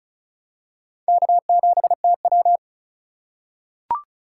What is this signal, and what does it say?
Morse code, heard as a keyed tone, about 36 wpm, 690 Hz. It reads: K7TW
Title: USA Call Signs - 4 Characters - 2x Character Spacing - 35wpm